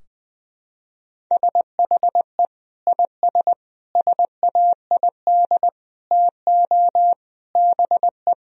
Morse code, heard as a keyed tone, about 20 wpm, 700 Hz. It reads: SHE IS SAID TO BE
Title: Sentences from Top 1000 Words - Review Part 2 - 20wpm